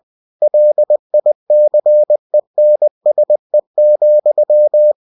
Morse code, heard as a keyed tone, about 20 wpm, 600 Hz. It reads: LICENSE,
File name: cw-books / winnie-the-pooh / winnie-20wpm-page-0112